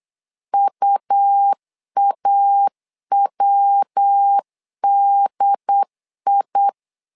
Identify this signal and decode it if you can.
Morse code, heard as a keyed tone, about 8 wpm, 800 Hz. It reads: UAWDI
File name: LEZIONI / 3 MWRO / 3 MW 8wpm 2501